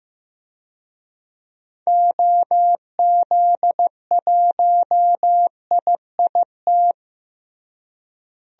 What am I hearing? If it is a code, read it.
Morse code, heard as a keyed tone, about 15 wpm, 700 Hz. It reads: OZ1IIT
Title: DXCC-100 Call Signs - 6 Characters - Repeated 3x - 15wpm